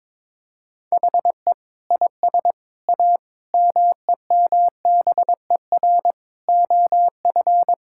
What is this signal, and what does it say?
Morse code, heard as a keyed tone, about 22 wpm, 700 Hz. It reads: HE IS A MEMBER OF
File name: practice / 22wpm / Sentences with 3-Word Phrases - 22wpm